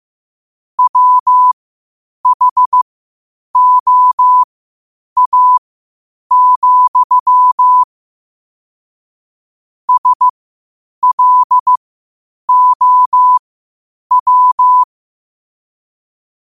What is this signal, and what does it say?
Morse code, heard as a keyed tone, about 15 wpm, spaced as for 8 wpm, 1000 Hz. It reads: WHOA, SLOW
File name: qotdfiles / 8wpm